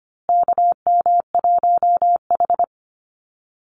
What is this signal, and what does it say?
Morse code, heard as a keyed tone, about 25 wpm, 700 Hz. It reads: KM1H
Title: USA Call Signs Repeated 3x 25wpm